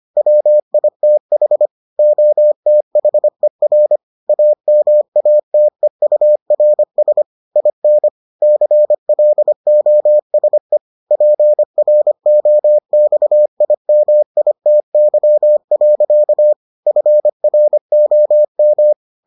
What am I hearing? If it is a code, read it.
Morse code, heard as a keyed tone, about 25 wpm, 600 Hz. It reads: WITH OTHER AMATEURS IN CLOSE PROXIMITY. FROM